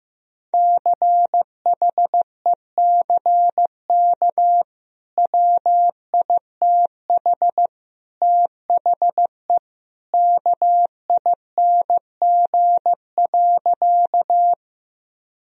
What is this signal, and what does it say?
Morse code, heard as a keyed tone, about 15 wpm, 700 Hz. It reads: CHECK WITH THE KING.
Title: Sentences from Top 1000 Words - Review Part 3 - 15wpm